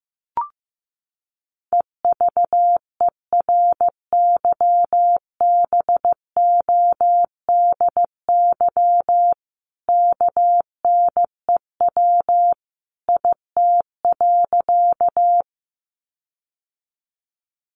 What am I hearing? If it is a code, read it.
Morse code, heard as a keyed tone, about 14 wpm, 700 Hz. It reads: EVERYBODY KNEW IT.
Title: Sentences from Top 1400 Words 15wpm